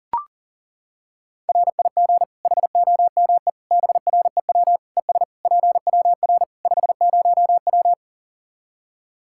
Morse code, heard as a keyed tone, about 40 wpm, 700 Hz. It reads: RIG HOME BREW ES PWR 50W
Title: Practice QSO - Exchange 2 - Info - 40wpm